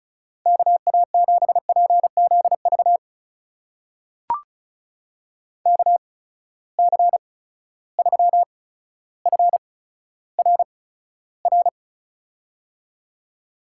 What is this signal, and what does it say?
Morse code, heard as a keyed tone, about 36 wpm, spaced as for 15 wpm, 700 Hz. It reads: KA7PZV KC3FRR
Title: USA Call Signs - 6 Characters - 8x Character Spacing - 35wpm